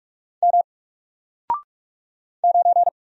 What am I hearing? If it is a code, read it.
Morse code, heard as a keyed tone, about 42 wpm, 700 Hz. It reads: M 9
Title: Single Character (Letter:Number:Punctuation:Prosign) Alt 45wpm